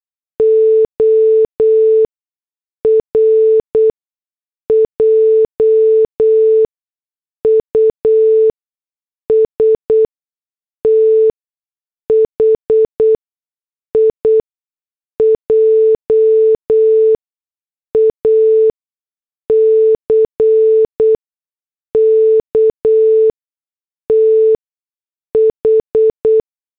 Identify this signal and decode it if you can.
Morse code, heard as a keyed tone, about 8 wpm, spaced as for 6 wpm, 440 Hz. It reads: OR JUST HIJACK TH